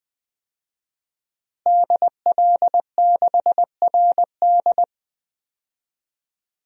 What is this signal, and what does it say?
Morse code, heard as a keyed tone, about 20 wpm, 700 Hz. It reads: DL6RD
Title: DXCC-100 Call Signs - 5 Characters - Repeated 2x - 20wpm